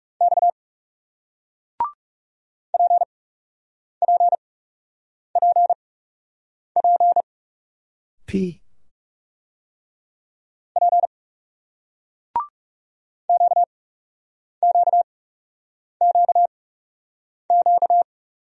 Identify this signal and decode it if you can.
Morse code, heard as a keyed tone, about 37 wpm, spaced as for 8 wpm, 700 Hz. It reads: X PPPP P QQQQ
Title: Speed-Racing Single Letters 30wpm